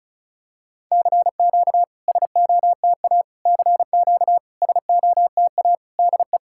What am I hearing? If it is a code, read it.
Morse code, heard as a keyed tone, about 34 wpm, 700 Hz. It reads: CQ SOTA CQ SOTA DE
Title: SOTA QSO Practice 35wpm